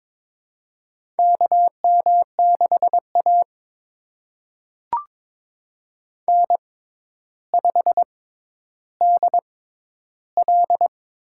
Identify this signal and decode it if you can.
Morse code, heard as a keyed tone, about 22 wpm, spaced as for 11 wpm, 700 Hz. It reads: KM6A N5DL
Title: USA Call Signs - 4 Characters - 6x Character Spacing - 22wpm